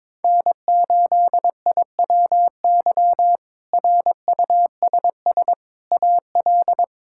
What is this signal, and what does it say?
Morse code, heard as a keyed tone, about 22 wpm, 700 Hz. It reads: N8IWY RUSS AL
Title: K1USN SST QSO Practice - 22wpm